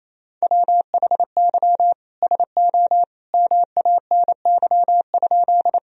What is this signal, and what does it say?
Morse code, heard as a keyed tone, about 28 wpm, 700 Hz. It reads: WHY SO MANY?
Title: Sentences from Top 1000 Words - Review Part 1 - 28wpm